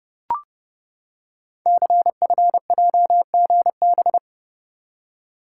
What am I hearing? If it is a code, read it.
Morse code, heard as a keyed tone, about 29 wpm, 700 Hz. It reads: CFJGB
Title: Code Groups - 5 Letters - 30wpm